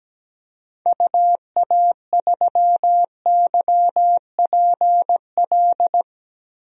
Morse code, heard as a keyed tone, about 17 wpm, 700 Hz. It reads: UA3YPL
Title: DXCC-100 Call Signs - 6 Characters - 17wpm